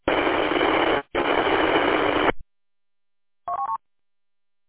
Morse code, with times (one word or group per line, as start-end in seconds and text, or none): none